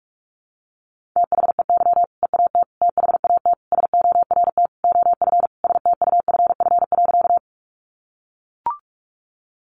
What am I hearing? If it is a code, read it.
Morse code, heard as a keyed tone, about 44 wpm, 700 Hz. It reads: THEY EAT THAT SORT OF STUFF.
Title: Sentences from Top 1400 Words 45wpm